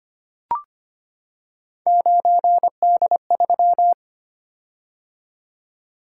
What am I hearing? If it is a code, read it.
Morse code, heard as a keyed tone, about 26 wpm, 700 Hz. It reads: 9D3
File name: Three Letter-Number Combinations - 25wpm